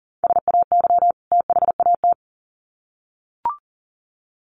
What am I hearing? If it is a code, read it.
Morse code, heard as a keyed tone, about 40 wpm, 700 Hz. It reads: SAY THAT
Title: Sets of 2 Words 40wpm